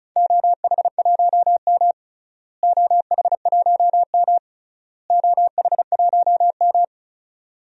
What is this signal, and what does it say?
Morse code, heard as a keyed tone, about 34 wpm, 700 Hz. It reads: OH1M OH1M OH1M
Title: DXCC-100 Call Signs - 4 Characters - Repeated 3x - 35wpm